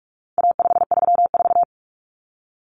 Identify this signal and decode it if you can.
Morse code, heard as a keyed tone, about 44 wpm, 700 Hz. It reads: A534
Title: Common Cut Numbers - 4-Digit 45wpm